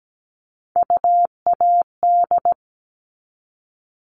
Morse code, heard as a keyed tone, about 17 wpm, 700 Hz. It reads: UAD
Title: All Cut Numbers - 3-Digit 17wpm